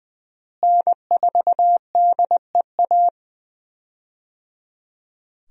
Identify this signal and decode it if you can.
Morse code, heard as a keyed tone, about 20 wpm, 700 Hz. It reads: N4DEA
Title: USA Call Signs - 5 Characters - Repeated 3x - 20wpm